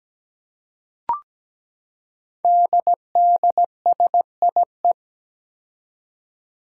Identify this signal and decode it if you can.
Morse code, heard as a keyed tone, about 17 wpm, 700 Hz. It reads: DDSIE
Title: Five Challenging Letter-Number Combinations 17wpm